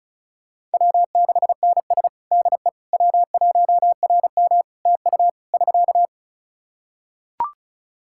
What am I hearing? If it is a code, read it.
Morse code, heard as a keyed tone, about 34 wpm, 700 Hz. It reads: W6NS DE W1RM TU <SK>
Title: K1USN SST QSO Practice 35wpm